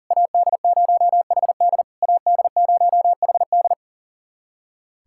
Morse code, heard as a keyed tone, about 37 wpm, 700 Hz. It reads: AD0HD AD0HD
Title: SOTA QSO Practice 40wpm